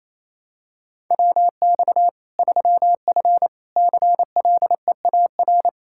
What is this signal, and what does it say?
Morse code, heard as a keyed tone, about 28 wpm, 700 Hz. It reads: WX 3F CLEAR
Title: K1USN SST QSO Practice 28wpm